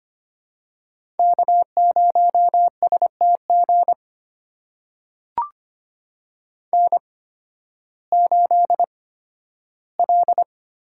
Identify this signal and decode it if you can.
Morse code, heard as a keyed tone, about 25 wpm, spaced as for 13 wpm, 700 Hz. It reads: K0STG N8L
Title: USA Call Signs - 8x Character Spacing - 25wpm